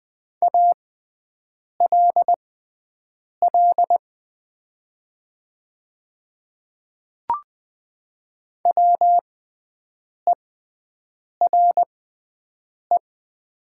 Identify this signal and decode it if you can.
Morse code, heard as a keyed tone, about 20 wpm, spaced as for 7 wpm, 690 Hz. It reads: ALL WERE
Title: 6x Character Spacing - Top 100 Words - 20wpm